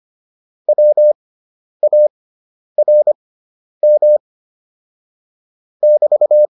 Morse code, heard as a keyed tone, about 25 wpm, spaced as for 10 wpm, 600 Hz. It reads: WARM <BT>